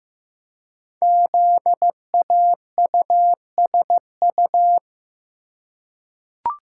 Morse code, heard as a keyed tone, about 15 wpm, 700 Hz. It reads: ZAUSU
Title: Code Groups - 5 Letters - 15wpm